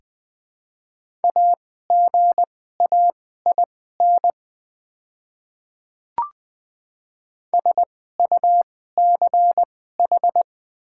Morse code, heard as a keyed tone, about 20 wpm, spaced as for 14 wpm, 700 Hz. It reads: AGAIN SUCH
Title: 2x Character Spacing - Top 200 Words - 20wpm